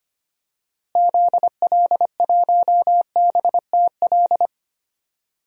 Morse code, heard as a keyed tone, about 25 wpm, 700 Hz. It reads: ZL1BTL
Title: DXCC-100 Call Signs - 6 Characters - Repeated 3x - 25wpm